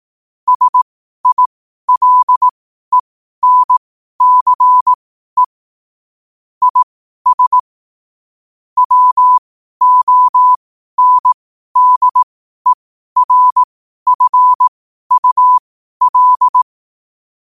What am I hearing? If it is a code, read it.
Morse code, heard as a keyed tone, about 18 wpm, spaced as for 12 wpm, 1000 Hz. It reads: SILENCE IS WONDERFUL
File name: 12wpm